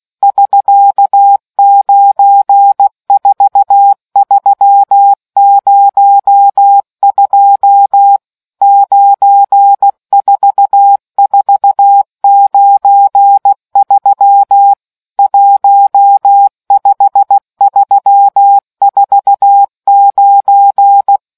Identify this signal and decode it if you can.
Morse code, heard as a keyed tone, about 16 wpm, 800 Hz. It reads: <SK> 94302 94493 15349